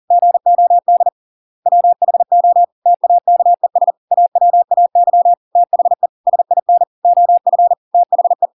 Morse code, heard as a keyed tone, about 40 wpm, 700 Hz. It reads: GOD WHO TAKES AWAY THE SIN OF THE